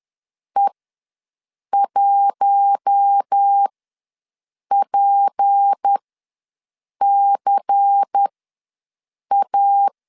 Morse code, heard as a keyed tone, about 10 wpm, spaced as for 6 wpm, 800 Hz. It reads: E1PCA
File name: LEZIONI / 6 PKX0 / 6 P ws 10wpm 3751